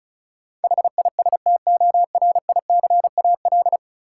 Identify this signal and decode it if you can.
Morse code, heard as a keyed tone, about 36 wpm, 700 Hz. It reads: HISTORICAL
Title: Sentences from Top 600 Words - New Word Drill - 35wpm